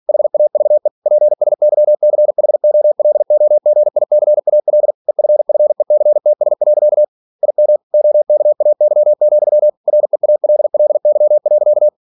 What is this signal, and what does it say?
Morse code, heard as a keyed tone, about 46 wpm, 600 Hz. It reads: HAVE PSYCHOLOGICAL EFFECTS. IM OKAY, REALLY.